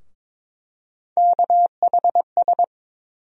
Morse code, heard as a keyed tone, about 22 wpm, 700 Hz. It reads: KHS
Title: USA Call Sign Postfixes - 3 Letters 22wpm